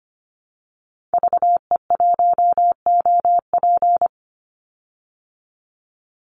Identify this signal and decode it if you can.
Morse code, heard as a keyed tone, about 25 wpm, 700 Hz. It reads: VE1OP
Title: Call Signs Repeated 2x - 25wpm